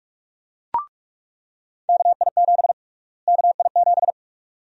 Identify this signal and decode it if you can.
Morse code, heard as a keyed tone, about 42 wpm, 700 Hz. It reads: KI7 KI7
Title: USA Call Sign Prefixes - Repeated 2x - 45wpm